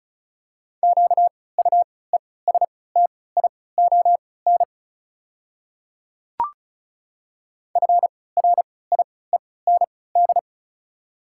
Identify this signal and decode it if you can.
Morse code, heard as a keyed tone, about 36 wpm, spaced as for 20 wpm, 690 Hz. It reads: QUESTION FRIEND
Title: 3x Character Spacing - Top 500 Words - 35wpm